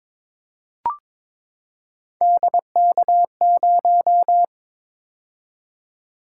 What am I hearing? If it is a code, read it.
Morse code, heard as a keyed tone, about 22 wpm, 700 Hz. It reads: DK0
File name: practice / 22wpm / Three Letter-Number Combinations - 22wpm